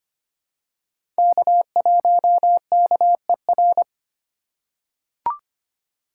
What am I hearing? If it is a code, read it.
Morse code, heard as a keyed tone, about 25 wpm, 700 Hz. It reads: K1KER
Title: USA Call Signs - 5 Characters - 3x Character Spacing - 25wpm